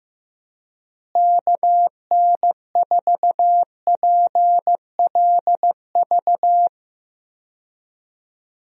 Morse code, heard as a keyed tone, about 15 wpm, 700 Hz. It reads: KN4PLV
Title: USA Call Signs - 6 Characters - Repeated 3x - 15wpm